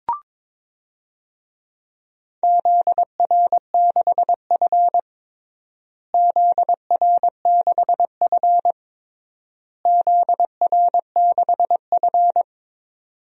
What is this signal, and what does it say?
Morse code, heard as a keyed tone, about 21 wpm, 700 Hz. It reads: ZR6F ZR6F ZR6F
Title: DXCC-100 Call Signs - 4 Characters - Repeated 3x - 22wpm